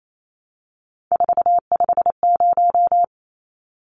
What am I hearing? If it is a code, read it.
Morse code, heard as a keyed tone, about 28 wpm, 700 Hz. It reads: 450
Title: Numbers - 3-Digit 28wpm